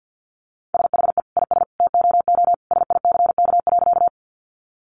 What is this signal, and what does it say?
Morse code, heard as a keyed tone, about 50 wpm, 700 Hz. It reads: SHE IS TOO SICK.
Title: Easy Phrases 50wpm